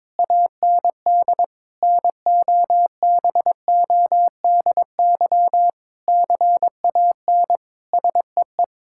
Morse code, heard as a keyed tone, about 22 wpm, 700 Hz. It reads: AND NOBODY CAN SEE
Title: Sentences from Top 1600 Words 22wpm